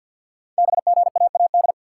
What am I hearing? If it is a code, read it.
Morse code, heard as a keyed tone, about 48 wpm, 700 Hz. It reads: BGAAD
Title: Code Groups - 5 Letters - 50wpm